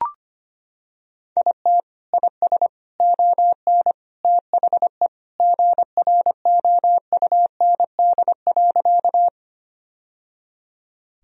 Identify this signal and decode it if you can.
Morse code, heard as a keyed tone, about 25 wpm, 700 Hz. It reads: IT IS ON THE GROUND.
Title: Sentences from Top 1000 Words - Review Part 2 - 25wpm